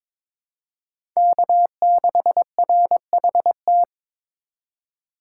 Morse code, heard as a keyed tone, about 22 wpm, 700 Hz. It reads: K6RHT
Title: USA Call Signs - 5 Characters - 2x Character Spacing - 22wpm